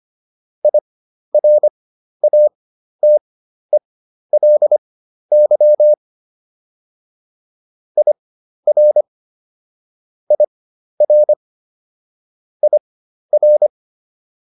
Morse code, heard as a keyed tone, about 25 wpm, spaced as for 12 wpm, 600 Hz. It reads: IRATELY IR IR IR